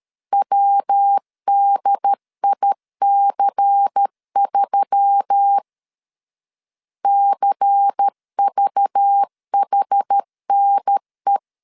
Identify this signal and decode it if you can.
Morse code, heard as a keyed tone, about 13 wpm, 800 Hz. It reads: WDIC3 CVHNE
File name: LEZIONI / 6 PKX0 / 6 P 12wpm 2033